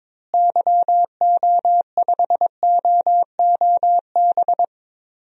Morse code, heard as a keyed tone, about 22 wpm, 700 Hz. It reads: YO5OOB
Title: DXCC-100 Call Signs - 6 Characters - Repeated 3x - 22wpm